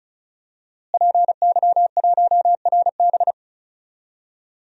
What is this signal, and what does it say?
Morse code, heard as a keyed tone, about 34 wpm, 700 Hz. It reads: PY1RB
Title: DXCC-100 Call Signs - 5 Characters - Repeated 2x - 35wpm